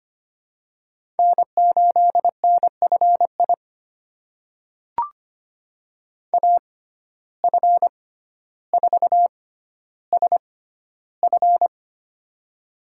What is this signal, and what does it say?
Morse code, heard as a keyed tone, about 25 wpm, spaced as for 13 wpm, 700 Hz. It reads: N8NFI AF4SF
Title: USA Call Signs - 5 Characters - 6x Character Spacing - 25wpm